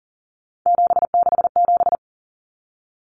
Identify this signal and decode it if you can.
Morse code, heard as a keyed tone, about 40 wpm, 700 Hz. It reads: 767
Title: Numbers - 3-Digit 40wpm